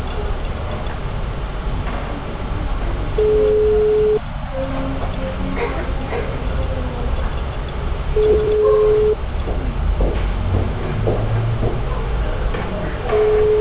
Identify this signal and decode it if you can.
Ringback tone (Europe, 425 Hz), with noise.